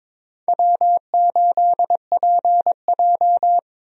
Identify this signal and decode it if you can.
Morse code, heard as a keyed tone, about 22 wpm, 700 Hz. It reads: W8PJ
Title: USA Call Signs - 4 Characters - Repeated 3x - 22wpm